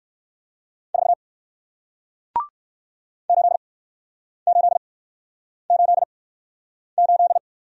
Morse code, heard as a keyed tone, about 60 wpm, spaced as for 9 wpm, 700 Hz. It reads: 4 8888